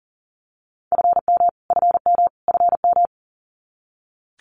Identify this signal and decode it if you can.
Morse code, heard as a keyed tone, about 40 wpm, 700 Hz. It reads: FM FM FM
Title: Abbreviations Repeated 3x - 40wpm